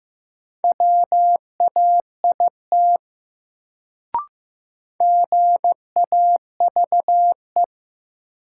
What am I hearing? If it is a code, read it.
Morse code, heard as a keyed tone, about 15 wpm, 700 Hz. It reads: WAIT GAVE
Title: Sentences from Top 500 Words - New Word Drill - 15wpm